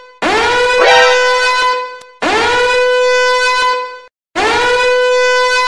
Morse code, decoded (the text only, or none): none